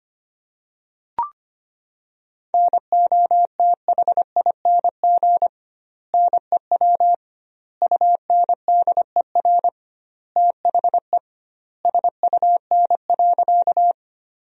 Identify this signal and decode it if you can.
Morse code, heard as a keyed tone, about 25 wpm, 700 Hz. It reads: NOTHING NEW UNDER THE SUN.
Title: 2x Word-Spacing - Sentences from Top 500 Words - 25wpm